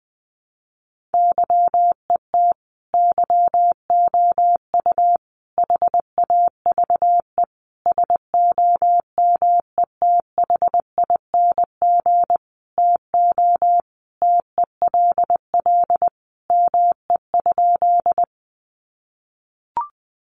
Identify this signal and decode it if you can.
Morse code, heard as a keyed tone, about 20 wpm, 700 Hz. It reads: YET YOU HAVE SOMETHING TO TELL ME?
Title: Sentences from Top 600 Words 20wpm